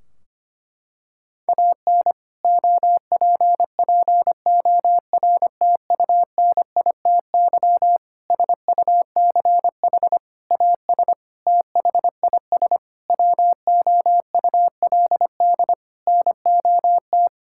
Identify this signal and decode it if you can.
Morse code, heard as a keyed tone, about 25 wpm, 700 Hz. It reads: AN OPPORTUNITY SUCH AS THIS WOULD NOT
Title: Sentences from Top 1600 Words 25wpm